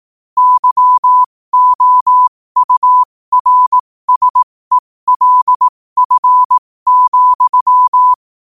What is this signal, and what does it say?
Morse code, heard as a keyed tone, about 18 wpm, spaced as for 15 wpm, 1000 Hz. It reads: YOURSELF,
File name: 15wpm